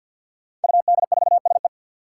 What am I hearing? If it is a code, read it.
Morse code, heard as a keyed tone, about 53 wpm, 700 Hz. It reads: UD4SE
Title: Five Challenging Letter-Number Combinations 50wpm